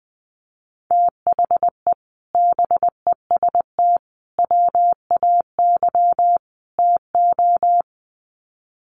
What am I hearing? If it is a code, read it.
Morse code, heard as a keyed tone, about 20 wpm, 700 Hz. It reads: THE BEST WAY TO
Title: Sets of 4 Words 20wpm